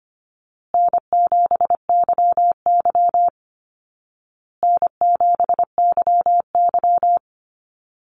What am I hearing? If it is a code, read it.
Morse code, heard as a keyed tone, about 25 wpm, 700 Hz. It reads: N7YY N7YY
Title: Call Signs Repeated 2x - 25wpm